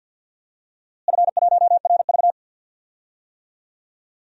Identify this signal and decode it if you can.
Morse code, heard as a keyed tone, about 47 wpm, 700 Hz. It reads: F1RV
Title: DXCC-100 Call Signs - 4 Characters - Repeated 3x - 50wpm